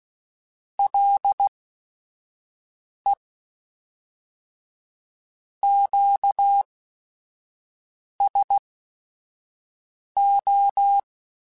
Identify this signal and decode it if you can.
Morse code, heard as a keyed tone, about 16 wpm, spaced as for 5 wpm, 800 Hz. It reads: LE QSO
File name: QSOs / QSOs-mp3s / WD0BGP 2@5